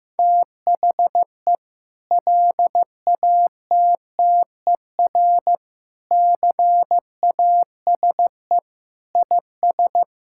Morse code, heard as a keyed tone, about 16 wpm, 700 Hz. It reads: THE LATTER CASE IS
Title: Sentences from Top 1600 Words 15wpm